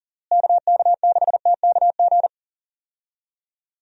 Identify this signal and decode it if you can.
Morse code, heard as a keyed tone, about 40 wpm, 700 Hz. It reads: KK6TKG
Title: USA Call Signs - 6 Characters - Repeated 3x - 40wpm